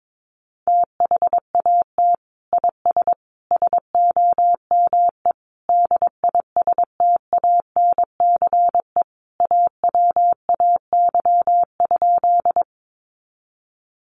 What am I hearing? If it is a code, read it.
Morse code, heard as a keyed tone, about 22 wpm, 700 Hz. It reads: THAT IS SOME DISTANCE AWAY?
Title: Sentences from Top 1200 Words 22wpm